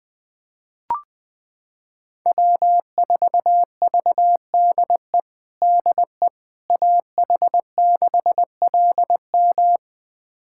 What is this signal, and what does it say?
Morse code, heard as a keyed tone, about 20 wpm, 700 Hz. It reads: W4VDE DE AH6LM
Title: USA Callsign DE Callsign - 5 Characters - 20wpm